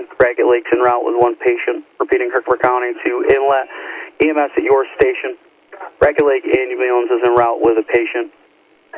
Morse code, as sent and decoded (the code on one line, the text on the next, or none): none